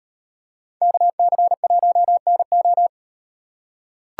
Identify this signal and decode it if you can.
Morse code, heard as a keyed tone, about 36 wpm, 700 Hz. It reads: KC1NO